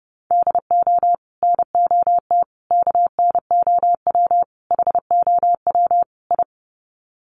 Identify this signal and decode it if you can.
Morse code, heard as a keyed tone, about 30 wpm, 700 Hz. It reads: DO NOT KNOW HOW I
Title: Sets of 5 Words Encore2 30wpm